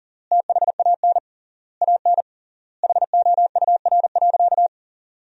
Morse code, heard as a keyed tone, about 40 wpm, 700 Hz. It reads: THAN AN HOUR.
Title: 3x Word-Spacing - Sentences from Top 500 Words - 40wpm